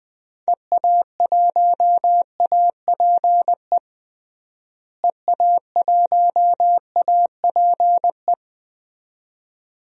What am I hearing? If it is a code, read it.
Morse code, heard as a keyed tone, about 21 wpm, 700 Hz. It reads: EA1APE EA1APE
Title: DXCC-100 Call Signs - 6 Characters - Repeated 2x - 20wpm